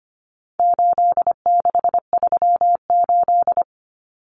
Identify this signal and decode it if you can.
Morse code, heard as a keyed tone, about 25 wpm, 700 Hz. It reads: 8638
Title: Common Cut Numbers - 4-Digit 25wpm